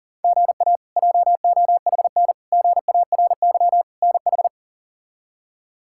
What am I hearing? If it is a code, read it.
Morse code, heard as a keyed tone, about 37 wpm, 700 Hz. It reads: GA JOHN GARY NH
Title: K1USN SST QSO Practice - 40wpm